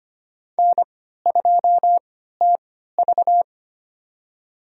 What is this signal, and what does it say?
Morse code, heard as a keyed tone, about 25 wpm, spaced as for 14 wpm, 700 Hz. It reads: N2TV
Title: USA Call Signs - 4 Characters - 3x Character Spacing - 25wpm